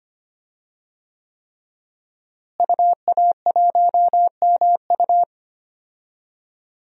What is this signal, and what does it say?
Morse code, heard as a keyed tone, about 25 wpm, 700 Hz. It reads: UA1MU
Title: DXCC-100 Call Signs - 5 Characters - Repeated 3x - 25wpm